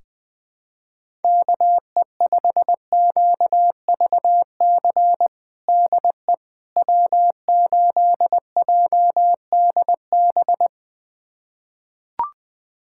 Morse code, heard as a keyed tone, about 21 wpm, 700 Hz. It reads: KE5QVC DE W8JDB
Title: USA Callsign DE Callsign 20wpm